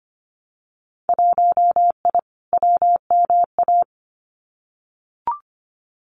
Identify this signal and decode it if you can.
Morse code, heard as a keyed tone, about 25 wpm, 700 Hz. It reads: 1I WMA
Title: Winter Field Day Exchanges Encore 25wpm